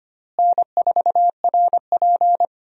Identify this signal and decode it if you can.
Morse code, heard as a keyed tone, about 25 wpm, 700 Hz. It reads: N4RP
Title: USA Call Signs - 4 Characters - Repeated 3x - 25wpm